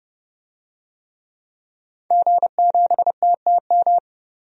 Morse code, heard as a keyed tone, about 32 wpm, 700 Hz. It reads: G7TTM
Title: DXCC-100 Call Signs - 5 Characters - 30wpm